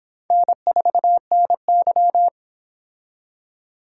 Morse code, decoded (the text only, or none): N4NY